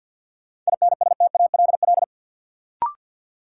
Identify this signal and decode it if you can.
Morse code, heard as a keyed tone, about 49 wpm, 700 Hz. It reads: INSTALL
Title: Sentences from Top 1600 Words - New Word Drill Alt - 50wpm